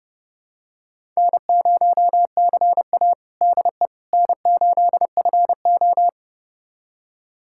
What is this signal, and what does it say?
Morse code, heard as a keyed tone, about 30 wpm, 700 Hz. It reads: N0CA DE N8FO
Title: USA Callsign DE Callsign - 4 Characters - 30wpm